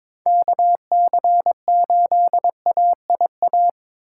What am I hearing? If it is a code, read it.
Morse code, heard as a keyed tone, about 22 wpm, 700 Hz. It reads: KC8AIA
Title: USA Call Signs - 6 Characters - 2x Character Spacing - 22wpm